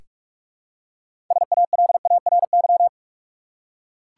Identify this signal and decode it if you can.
Morse code, heard as a keyed tone, about 42 wpm, 700 Hz. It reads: SALARY
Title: Six Letter Words - 45wpm